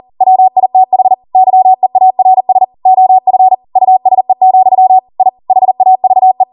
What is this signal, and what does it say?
Morse code, heard as a keyed tone, about 40 wpm, 750 Hz. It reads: WITH YEARS OF USE, I HAVE